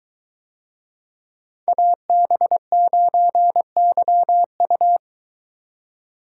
Morse code, heard as a keyed tone, about 22 wpm, 700 Hz. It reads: AB9YU